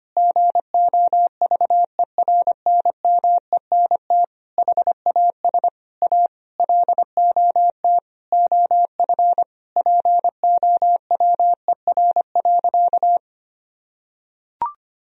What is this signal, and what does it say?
Morse code, heard as a keyed tone, about 25 wpm, 700 Hz. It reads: GOVERNMENT HAS A LOT OF POWER.
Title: Sentences from Top 1000 Words - Review Part 4 - 25wpm